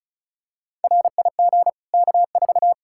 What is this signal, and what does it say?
Morse code, heard as a keyed tone, about 36 wpm, 700 Hz. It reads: RIG K4
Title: Practice QSO - Exchange 2 - Info - 35wpm